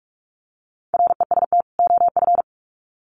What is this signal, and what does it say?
Morse code, heard as a keyed tone, about 44 wpm, 700 Hz. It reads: REST OF
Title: Sets of 2 Words 45wpm